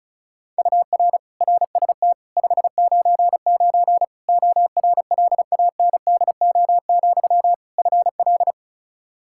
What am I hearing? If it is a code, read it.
Morse code, heard as a keyed tone, about 34 wpm, 700 Hz. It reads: UR RST 599 ORLANDO, FL
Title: Practice QSO - Exchange 1 - Beginning - 35wpm